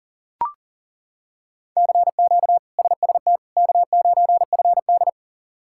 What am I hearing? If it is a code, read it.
Morse code, heard as a keyed tone, about 40 wpm, 700 Hz. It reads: CQ SST K9FD
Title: K1USN SST QSO Practice - 40wpm